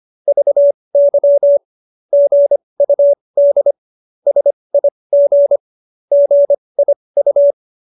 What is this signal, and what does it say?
Morse code, heard as a keyed tone, about 25 wpm, spaced as for 20 wpm, 570 Hz. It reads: VY GUD SIG GIU
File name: qso215b 20